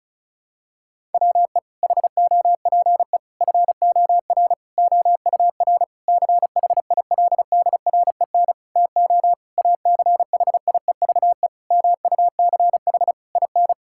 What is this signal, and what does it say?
Morse code, heard as a keyed tone, about 36 wpm, 690 Hz. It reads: WE HOPE FOR OUR CHILDREN TO ACHIEVE MUCH IN
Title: Sentences from Top 1000 Words - Review Part 6 - 35wpm